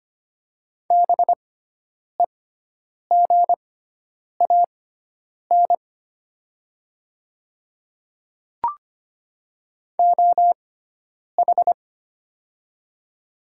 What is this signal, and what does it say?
Morse code, heard as a keyed tone, about 25 wpm, spaced as for 9 wpm, 700 Hz. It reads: BEGAN OH
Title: 6x Character Spacing - Top 500 Words - 25wpm